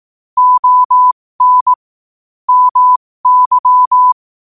Morse code, heard as a keyed tone, about 18 wpm, spaced as for 15 wpm, 1000 Hz. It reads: ON MY
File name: qotdfiles / 15wpm